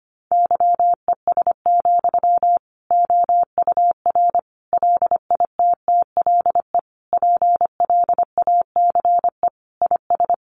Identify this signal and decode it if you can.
Morse code, heard as a keyed tone, about 25 wpm, 700 Hz. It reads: YES, OUR LITTLE PLACE IS
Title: Sentences from Top 700 Words 25wpm